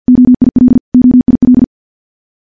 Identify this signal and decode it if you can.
Morse code, heard as a keyed tone, about 50 wpm, 250 Hz. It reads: OIL OIL
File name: morse / morse-training-50WPM